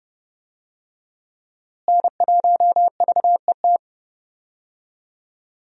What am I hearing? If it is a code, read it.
Morse code, heard as a keyed tone, about 31 wpm, 700 Hz. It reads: N1VET
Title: USA Call Signs - 5 Characters - Repeated 3x - 30wpm